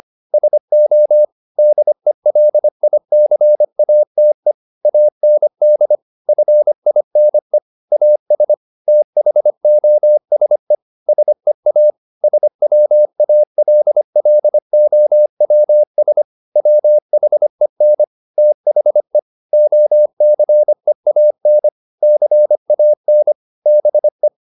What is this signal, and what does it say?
Morse code, heard as a keyed tone, about 25 wpm, 600 Hz. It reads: SO DELICATE AND FINE AS THOSE SEA SWALLOWS WHEN THE OCEAN CAN BE
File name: the old man and the sea part 1 25wpm